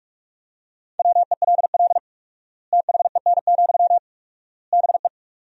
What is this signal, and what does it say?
Morse code, heard as a keyed tone, about 46 wpm, 700 Hz. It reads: WELL THEN, BE
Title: 4x Word-Spacing - Sentences from Top 500 Words - 45wpm